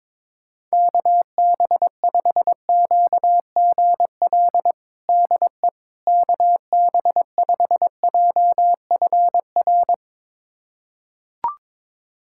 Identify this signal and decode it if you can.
Morse code, heard as a keyed tone, about 22 wpm, 700 Hz. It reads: KB5QGL DE KB5JFR
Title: USA Callsign DE Callsign - 6 Characters - 22wpm